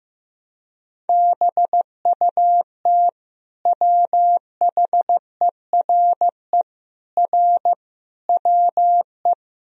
Morse code, heard as a keyed tone, about 15 wpm, 700 Hz. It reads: BUT WHERE R WE
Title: Sentences from Top 200 Words and Abbreviations 15wpm